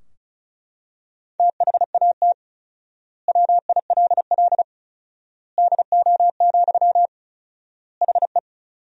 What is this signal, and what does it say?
Morse code, heard as a keyed tone, about 36 wpm, 700 Hz. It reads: THAT WILL DO, HE
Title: 4x Word-Spacing - Sentences from Top 100 Words - 35wpm